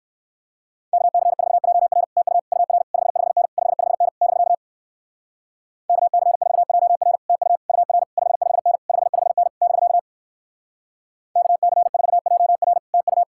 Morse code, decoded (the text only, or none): KC4JR TU UR 55N 55N <BK> KC4JR TU UR 55N 55N <BK> KC4JR TU